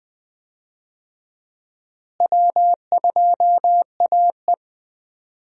Morse code, heard as a keyed tone, about 20 wpm, 700 Hz. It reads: W2AE
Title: USA Call Signs - 4 Characters - Repeated 2x - 20wpm